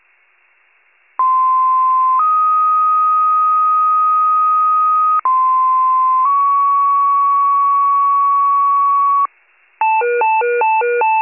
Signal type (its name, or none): two-tone page